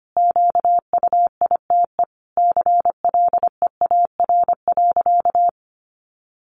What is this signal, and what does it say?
Morse code, heard as a keyed tone, about 25 wpm, 700 Hz. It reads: QUITE CLEAR.
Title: Sentences from Top 1100 Words 25wpm